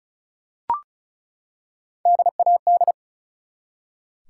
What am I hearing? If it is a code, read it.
Morse code, heard as a keyed tone, about 36 wpm, 700 Hz. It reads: DAD
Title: Three Letter Words - 35wpm